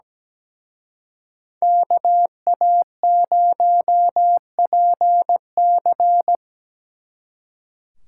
Morse code, heard as a keyed tone, about 20 wpm, 700 Hz. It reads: KA0PC
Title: Speed Racing - USA 2x2 Call Signs 17wpm